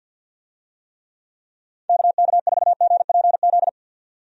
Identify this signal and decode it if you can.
Morse code, heard as a keyed tone, about 50 wpm, 700 Hz. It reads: KK4GPZ
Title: USA Call Signs - 6 Characters - 50wpm